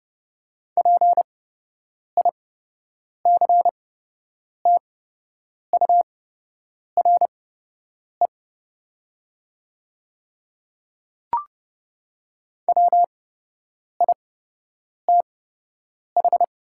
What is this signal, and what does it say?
Morse code, heard as a keyed tone, about 30 wpm, spaced as for 8 wpm, 700 Hz. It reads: PICTURE WITH
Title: 8x Character Spacing - Top 200 Words - 30wpm